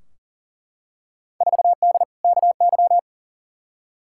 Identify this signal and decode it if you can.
Morse code, heard as a keyed tone, about 40 wpm, 700 Hz. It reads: 4D KY
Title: Field Day Exchanges Encore - 40wpm